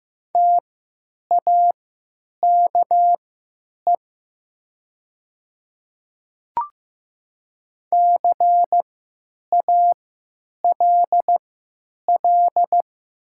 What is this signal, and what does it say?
Morse code, heard as a keyed tone, about 15 wpm, spaced as for 9 wpm, 700 Hz. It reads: TAKE CALL
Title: 3x Character Spacing - Top 300 Words - 15wpm